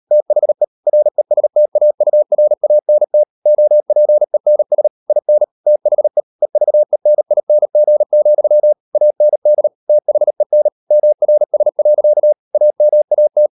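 Morse code, heard as a keyed tone, about 38 wpm, 600 Hz. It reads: THE RESTAURANT OPENS IN THE EVENING, AND THEN MRS. AMAT
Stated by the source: Morse code practice at 38 WPM, sending Ken Follett's novel Never in English, from CW med Gnister.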